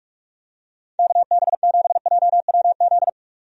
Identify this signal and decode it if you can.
Morse code, heard as a keyed tone, about 47 wpm, 700 Hz. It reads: KB7JWZ
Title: USA Call Signs - 6 Characters - 45wpm